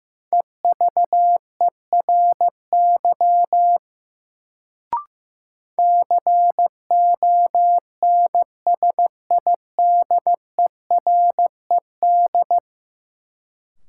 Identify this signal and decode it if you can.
Morse code, heard as a keyed tone, about 15 wpm, 700 Hz. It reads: EVERY CONSIDERED
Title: Sets of 5 Words - New Word Drill - 15wpm